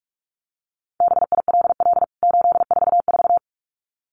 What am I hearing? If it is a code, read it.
Morse code, heard as a keyed tone, about 44 wpm, 700 Hz. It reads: BILL 844
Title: CWT Exchange 45wpm